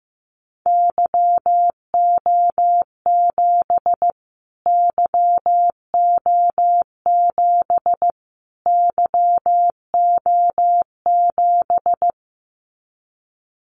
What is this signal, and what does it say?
Morse code, heard as a keyed tone, about 15 wpm, 700 Hz. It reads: YO7 YO7 YO7
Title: Callsign Prefixes for Top 25 DXCC - Repeated 3x - 15wpm